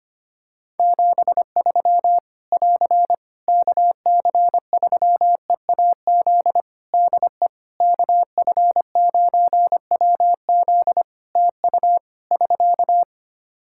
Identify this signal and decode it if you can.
Morse code, heard as a keyed tone, about 25 wpm, 700 Hz. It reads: 73 <AR> KC3EAZ DE KF9WZ TU <SK>
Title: Practice QSO - Exchange 4 - Ending - 25wpm